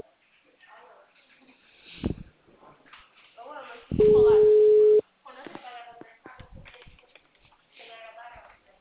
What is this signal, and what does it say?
Ringback tone (Europe, 425 Hz), with speech.